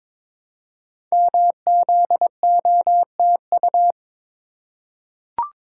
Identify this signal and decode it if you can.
Morse code, heard as a keyed tone, about 22 wpm, 700 Hz. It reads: MZOTU
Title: Code Groups - 5 Letters - 22wpm